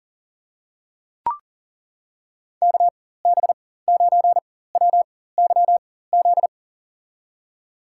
Morse code, heard as a keyed tone, about 39 wpm, spaced as for 18 wpm, 700 Hz. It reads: KB9WYZ
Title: USA Call Signs - 6 Characters - 4x Character Spacing - 40wpm